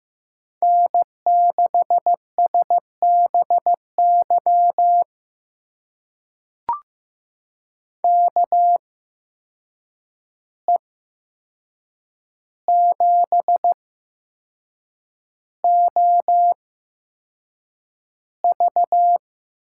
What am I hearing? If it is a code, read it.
Morse code, heard as a keyed tone, about 15 wpm, spaced as for 6 wpm, 700 Hz. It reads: N6SBY KE7OV
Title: USA Call Signs - 8x Character Spacing - 15wpm